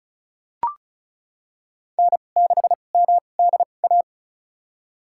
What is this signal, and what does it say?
Morse code, heard as a keyed tone, about 34 wpm, spaced as for 25 wpm, 700 Hz. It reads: N6MDA
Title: USA Call Signs - 2x Character Spacing - 35wpm